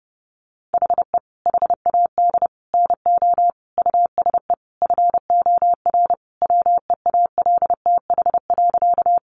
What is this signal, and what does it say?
Morse code, heard as a keyed tone, about 30 wpm, 700 Hz. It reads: HE HAD NO USE FOR WEALTH.
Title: Sentences from Top 1200 Words 30wpm